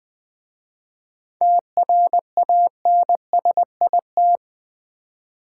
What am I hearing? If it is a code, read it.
Morse code, heard as a keyed tone, about 20 wpm, 690 Hz. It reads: TRANSIT
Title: 7 Letter Words - First 3 Letters Given - 20wpm